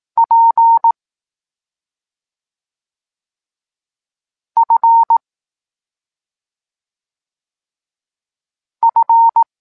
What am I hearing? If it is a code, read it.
Morse code, heard as a keyed tone, about 18 wpm, 920 Hz. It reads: P F F
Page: Tekst 25 – Nauka liter P F